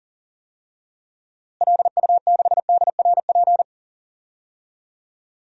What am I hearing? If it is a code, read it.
Morse code, heard as a keyed tone, about 38 wpm, 700 Hz. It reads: LU6DRP
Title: DXCC-100 Call Signs - 6 Characters - Repeated 3x - 40wpm